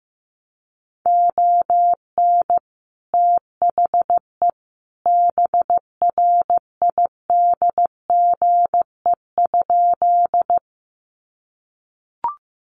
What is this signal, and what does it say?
Morse code, heard as a keyed tone, about 15 wpm, 700 Hz. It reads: ON THE BRIDGE?
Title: Sentences from Top 1200 Words 15wpm